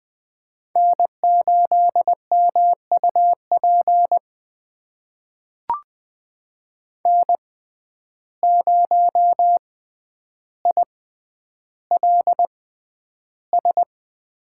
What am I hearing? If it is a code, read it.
Morse code, heard as a keyed tone, about 20 wpm, spaced as for 10 wpm, 700 Hz. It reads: N8MUP N0ILS
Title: USA Call Signs - 5 Characters - 6x Character Spacing - 20wpm